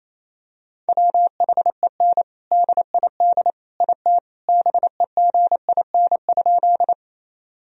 Morse code, heard as a keyed tone, about 28 wpm, 700 Hz. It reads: WHEN DID IT BEGIN?
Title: Sentences from Top 1000 Words - Review Part 1 - 28wpm